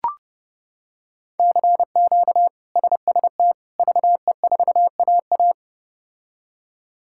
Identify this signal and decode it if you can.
Morse code, heard as a keyed tone, about 30 wpm, 700 Hz. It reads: CQ SST VE4AA
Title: K1USN SST QSO Practice - 30wpm